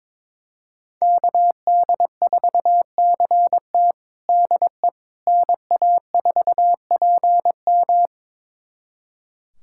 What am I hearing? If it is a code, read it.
Morse code, heard as a keyed tone, about 22 wpm, 700 Hz. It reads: KD4CT DE NA4PM
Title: USA Callsign DE Callsign - 5 Characters - 22wpm